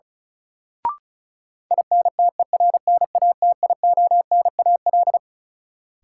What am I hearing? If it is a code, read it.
Morse code, heard as a keyed tone, about 36 wpm, 700 Hz. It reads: INTERNATIONAL
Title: Sentences from Top 1000 Words - Review Part 4 - New Word Drill - 35wpm